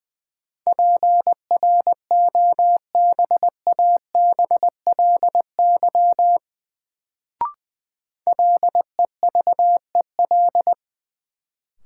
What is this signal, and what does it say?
Morse code, heard as a keyed tone, about 20 wpm, 700 Hz. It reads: PROBABLY LEVEL
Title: Sentences from Top 700 Words - New Word Drill - 20wpm